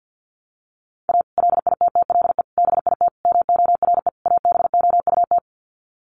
Morse code, heard as a keyed tone, about 50 wpm, 700 Hz. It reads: A LITTLE BIT MORE ABOUT
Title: Sets of 5 Words Encore2 50wpm